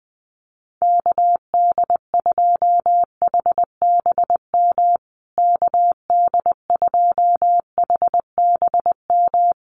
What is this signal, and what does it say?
Morse code, heard as a keyed tone, about 20 wpm, 700 Hz. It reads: KD2HBM KD2HBM
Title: USA Call Signs Repeated 3x 20wpm